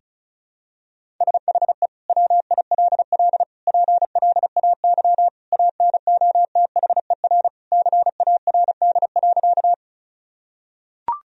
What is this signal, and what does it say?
Morse code, heard as a keyed tone, about 34 wpm, 700 Hz. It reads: SHE WILL PLAY ANOTHER CARD.
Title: Sentences from Top 1000 Words - Review Part 4 - 35wpm